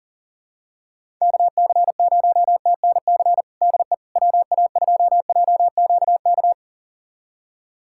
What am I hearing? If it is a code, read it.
Morse code, heard as a keyed tone, about 40 wpm, 700 Hz. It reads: KC0TNC DE WA2JQK
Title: USA Callsign DE Callsign - 6 Characters - 40wpm